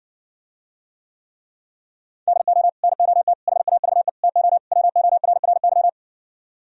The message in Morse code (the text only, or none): DO NOT HAVE TO WORRY